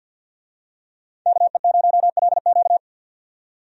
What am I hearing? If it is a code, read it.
Morse code, heard as a keyed tone, about 53 wpm, 700 Hz. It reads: KE0LQ
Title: USA Call Signs - 5 Characters - 50wpm